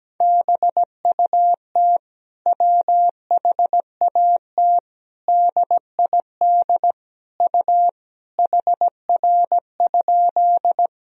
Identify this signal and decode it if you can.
Morse code, heard as a keyed tone, about 17 wpm, 700 Hz. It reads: BUT WHAT DID U HR?
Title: Sentences from Top 300 Words and Abbreviations 17wpm